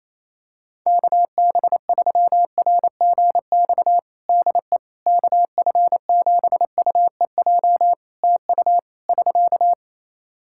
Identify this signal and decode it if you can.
Morse code, heard as a keyed tone, about 28 wpm, 700 Hz. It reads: KB3RGX DE KF7UEJ TU <SK>
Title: Practice QSO - Exchange 4 - Ending - 28wpm